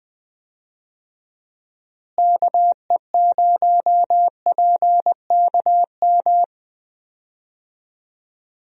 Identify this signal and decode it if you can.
Morse code, heard as a keyed tone, about 21 wpm, 700 Hz. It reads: KE0PKM
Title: USA Call Signs - 6 Characters - Repeated 3x - 20wpm